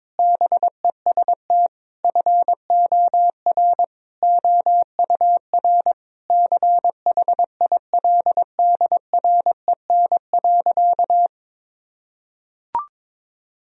Morse code, heard as a keyed tone, about 22 wpm, 700 Hz. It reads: BEST FOR OUR CHILDREN.
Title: Sentences from Top 1000 Words - Review Part 4 - 22wpm